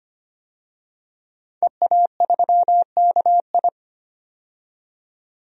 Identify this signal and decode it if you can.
Morse code, heard as a keyed tone, about 25 wpm, 700 Hz. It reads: EA3KI
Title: DXCC-100 Call Signs - 5 Characters - 25wpm